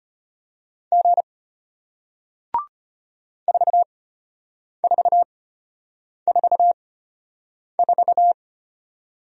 Morse code, heard as a keyed tone, about 33 wpm, spaced as for 8 wpm, 700 Hz. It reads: G 4444